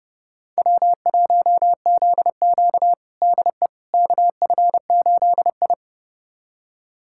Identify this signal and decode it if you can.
Morse code, heard as a keyed tone, about 30 wpm, 700 Hz. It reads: W1ZQ DE KF8I
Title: USA Callsign DE Callsign - 4 Characters - 30wpm